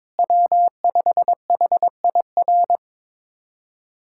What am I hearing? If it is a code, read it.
Morse code, heard as a keyed tone, about 21 wpm, 700 Hz. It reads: W5HIR
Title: USA Callsign DE Callsign 22wpm